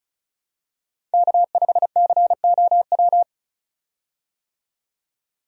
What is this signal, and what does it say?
Morse code, heard as a keyed tone, about 36 wpm, 700 Hz. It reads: K5COW
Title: USA Call Signs - 5 Characters - 35wpm